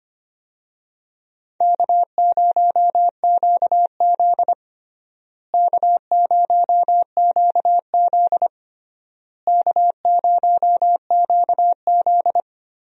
Morse code, heard as a keyed tone, about 25 wpm, 700 Hz. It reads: K0QZ K0QZ K0QZ
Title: USA Call Signs - 4 Characters - Repeated 3x - 25wpm